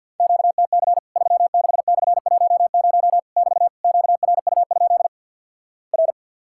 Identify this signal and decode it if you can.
Morse code, heard as a keyed tone, about 50 wpm, 700 Hz. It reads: QTC 36/10 <BT> QRU? R
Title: Worked All Europe DX Contest - QTC Exchange with Repeats - 50wpm